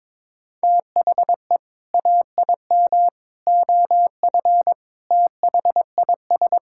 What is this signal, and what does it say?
Morse code, heard as a keyed tone, about 22 wpm, 700 Hz. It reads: THE AIM OF THIS
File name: Sentences with 7-Word Phrases - 22wpm